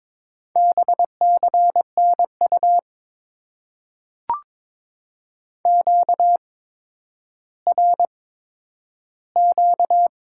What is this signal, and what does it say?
Morse code, heard as a keyed tone, about 22 wpm, spaced as for 11 wpm, 700 Hz. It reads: BCNU QRQ
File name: Top 90 QSO Elements - 8x Character Spacing - 22wpm